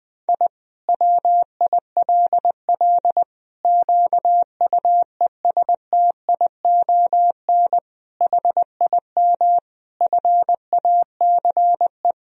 Morse code, heard as a keyed tone, about 20 wpm, 700 Hz. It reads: I WILL QUESTION HIM FACE
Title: Sentences from Top 1000 Words - Review Part 2 - 20wpm